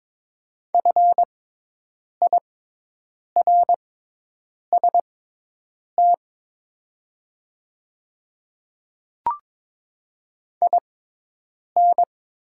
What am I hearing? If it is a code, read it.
Morse code, heard as a keyed tone, about 23 wpm, spaced as for 8 wpm, 690 Hz. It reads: FIRST IN
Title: 6x Character Spacing - Top 100 Words - 22wpm